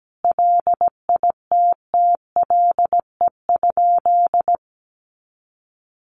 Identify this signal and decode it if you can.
Morse code, heard as a keyed tone, about 17 wpm, 700 Hz. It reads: LITTLE?
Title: Sentences from Top 500 Words 17wpm